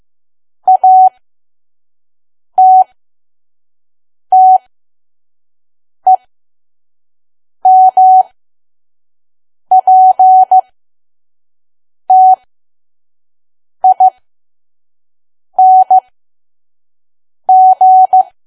Morse code, heard as a keyed tone, about 15 wpm, spaced as for 5 wpm, 750 Hz. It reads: ATTEMPTING
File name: Archive / 5 WPM files / PM